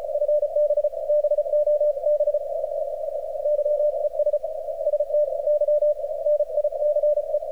Morse code, heard as a keyed tone, about 36 wpm, 600 Hz, with radio noise. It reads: FB BOB YES ST Y NICE
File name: archived / CRSnet